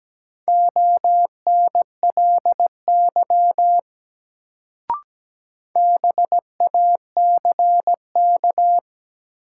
Morse code, heard as a keyed tone, about 17 wpm, 700 Hz. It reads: ONLY BACK
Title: Sentences from Top 200 Words - New Word Drill - 17wpm